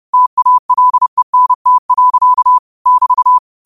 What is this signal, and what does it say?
Morse code, heard as a keyed tone, about 30 wpm, 1000 Hz. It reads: TALENT. <BT>
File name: qotdfiles / 30wpm